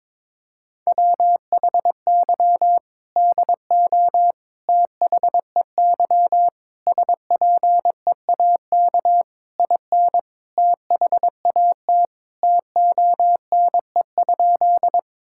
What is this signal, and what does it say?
Morse code, heard as a keyed tone, about 22 wpm, 700 Hz. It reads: WHY DO THEY SPEAK IN THAT TONE?
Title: Sentences from Top 1600 Words 22wpm